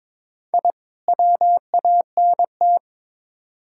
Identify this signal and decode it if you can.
Morse code, heard as a keyed tone, about 21 wpm, 700 Hz. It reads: I WANT
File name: Build A Phrase - 2-Word Phrases - 22wpm